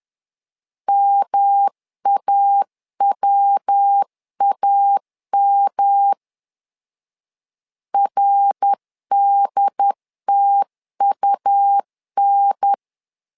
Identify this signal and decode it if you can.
Morse code, heard as a keyed tone, about 10 wpm, 800 Hz. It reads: MAWAM RDTUN
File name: LEZIONI / 3 MWRO / 3 MWR 10wpm 1952